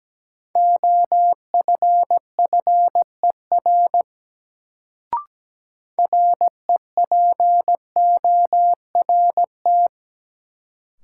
Morse code, heard as a keyed tone, about 17 wpm, 700 Hz. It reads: OFFER REPORT
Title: Sentences from Top 900 Words - New Word Drill - 17wpm